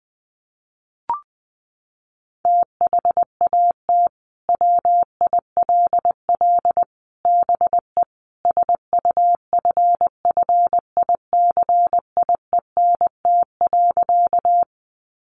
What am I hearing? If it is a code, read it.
Morse code, heard as a keyed tone, about 20 wpm, 700 Hz. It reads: THAT WILL BE SUFFICIENT.
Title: Sentences from Top 1400 Words 20wpm